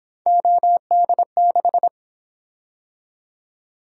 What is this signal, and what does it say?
Morse code, heard as a keyed tone, about 26 wpm, 700 Hz. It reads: OD6
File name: Speed-Racing - Three Letter-Number Combinations - 22wpm